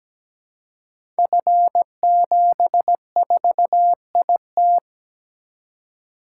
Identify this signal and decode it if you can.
Morse code, heard as a keyed tone, about 17 wpm, 700 Hz. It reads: F74IT
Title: Code Groups - 5 Letter:Numbers - 17wpm